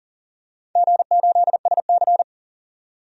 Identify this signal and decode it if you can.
Morse code, heard as a keyed tone, about 37 wpm, 700 Hz. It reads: G8SC
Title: DXCC-100 Call Signs - 4 Characters - 40wpm